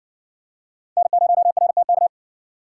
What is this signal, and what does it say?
Morse code, heard as a keyed tone, about 55 wpm, 700 Hz. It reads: N0RTK